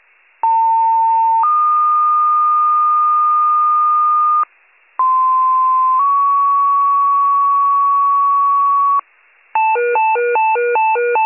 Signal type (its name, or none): two-tone page